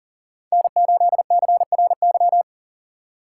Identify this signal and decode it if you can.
Morse code, heard as a keyed tone, about 40 wpm, 700 Hz. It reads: N8CRY
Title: USA Call Signs - 5 Characters - 3x Character Spacing - 40wpm